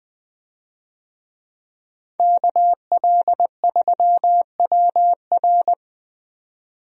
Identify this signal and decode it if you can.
Morse code, heard as a keyed tone, about 20 wpm, 700 Hz. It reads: KL3WR
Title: USA Call Signs - 5 Characters - 20wpm